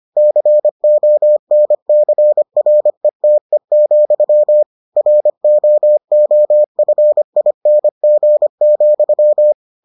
Morse code, heard as a keyed tone, about 25 wpm, 600 Hz. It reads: CONCRETE, ROOFING,